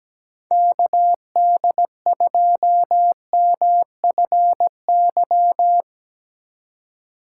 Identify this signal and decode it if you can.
Morse code, heard as a keyed tone, about 17 wpm, 700 Hz. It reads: KD2MFY
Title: USA Call Signs - 6 Characters - 4x Character Spacing - 17wpm